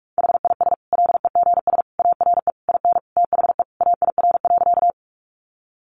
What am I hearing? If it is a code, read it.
Morse code, heard as a keyed tone, about 44 wpm, 700 Hz. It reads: HIS LEGS ARE IN THE AIR.
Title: Additional Easy Phrases 45wpm